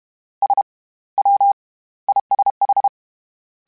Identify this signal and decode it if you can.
Morse code, heard as a keyed tone, about 32 wpm, 800 Hz. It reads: S W ISH